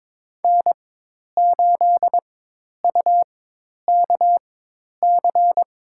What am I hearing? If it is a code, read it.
Morse code, heard as a keyed tone, about 22 wpm, spaced as for 10 wpm, 700 Hz. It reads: N8UKC
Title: USA Call Signs - 4x Character Spacing - 22wpm